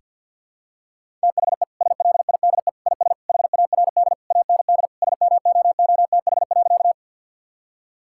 Morse code, heard as a keyed tone, about 50 wpm, 700 Hz. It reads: THE SLIDE IS HARD AND SMOOTH.
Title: Sentences from Top 1600 Words 50wpm